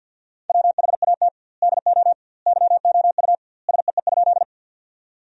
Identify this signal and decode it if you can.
Morse code, heard as a keyed tone, about 52 wpm, 700 Hz. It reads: WHAT DO YOU SEE?
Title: 2x Word-Spacing - Sentences from Top 100 Words - 50wpm